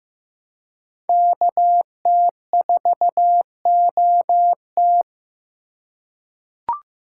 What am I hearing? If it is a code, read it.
Morse code, heard as a keyed tone, about 15 wpm, 700 Hz. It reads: KT4OT
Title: USA Call Signs - 5 Characters - 2x Character Spacing - 15wpm